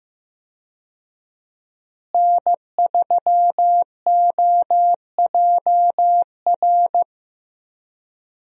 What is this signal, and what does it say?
Morse code, heard as a keyed tone, about 16 wpm, 700 Hz. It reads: N3OJR
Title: USA Call Signs - 5 Characters - Repeated 2x - 15wpm